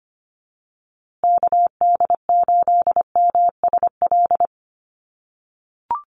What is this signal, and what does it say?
Morse code, heard as a keyed tone, about 25 wpm, 700 Hz. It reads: KD8MSL
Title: USA Call Signs Repeated 3x 25wpm